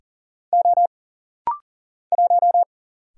Morse code, heard as a keyed tone, about 40 wpm, 700 Hz. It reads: O 1
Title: Single Letter-Number Alt - Rapid-Fire 40wpm